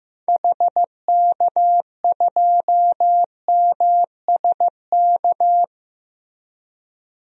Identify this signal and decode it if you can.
Morse code, heard as a keyed tone, about 15 wpm, 700 Hz. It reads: HK2MSK
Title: DXCC-100 Call Signs - 6 Characters - Repeated 3x - 15wpm